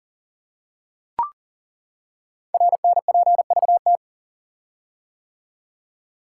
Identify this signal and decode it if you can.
Morse code, heard as a keyed tone, about 42 wpm, 700 Hz. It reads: RNPVT
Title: Code Groups - 5 Letters - 40wpm